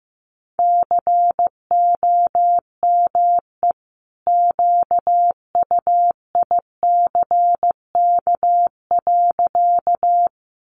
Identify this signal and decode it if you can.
Morse code, heard as a keyed tone, about 15 wpm, 700 Hz. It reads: COME QUICK.
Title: Sentences from Top 500 Words 15wpm